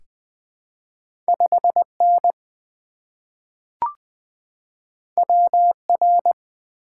Morse code, heard as a keyed tone, about 20 wpm, 700 Hz. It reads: 5 N W R
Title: Two Letter-Number Combinations - 20wpm